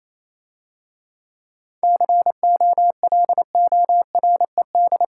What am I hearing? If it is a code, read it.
Morse code, heard as a keyed tone, about 27 wpm, 700 Hz. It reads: COLORED
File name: Easy Words - Speed-Racing with Hints - 22wpm